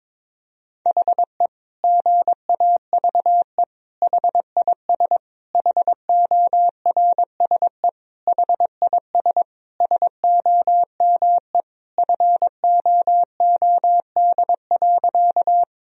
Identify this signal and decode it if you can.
Morse code, heard as a keyed tone, about 22 wpm, 700 Hz. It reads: HE GAVE HIS HORSE HIS SOME FOOD.
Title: Sentences from Top 1000 Words - Review Part 1 - 22wpm